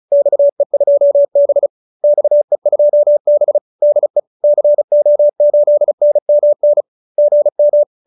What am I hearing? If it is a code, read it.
Morse code, heard as a keyed tone, about 34 wpm, 590 Hz. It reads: XE2B XE2B DE CO8NMN GM